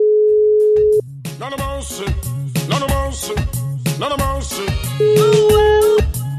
Ringback tone (Europe, 425 Hz).